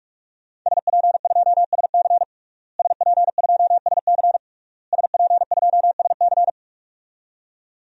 Morse code, heard as a keyed tone, about 46 wpm, 700 Hz. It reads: SP2SC SP2SC SP2SC
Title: DXCC-100 Call Signs - 5 Characters - Repeated 3x - 45wpm